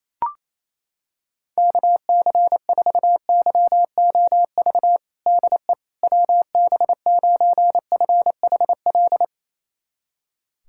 Morse code, heard as a keyed tone, about 28 wpm, 700 Hz. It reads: KC4YOV DE WB9FHL
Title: USA Callsign DE Callsign - 6 Characters - 28wpm